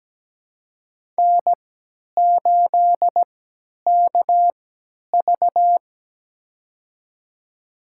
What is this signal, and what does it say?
Morse code, heard as a keyed tone, about 17 wpm, spaced as for 10 wpm, 700 Hz. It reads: N8KV
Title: USA Call Signs - 4 Characters - 3x Character Spacing - 17wpm